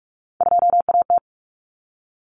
Morse code, heard as a keyed tone, about 44 wpm, 700 Hz. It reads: 2AT
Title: Common Cut Numbers - 3-Digit 45wpm